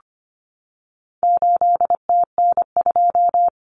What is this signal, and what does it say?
Morse code, heard as a keyed tone, about 25 wpm, 700 Hz. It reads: 8TN2
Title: Common Cut Numbers - 4-Digit 25wpm